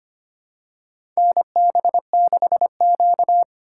Morse code, heard as a keyed tone, about 25 wpm, 700 Hz. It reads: NB6Q
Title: USA Call Signs - 2x Character Spacing - 25wpm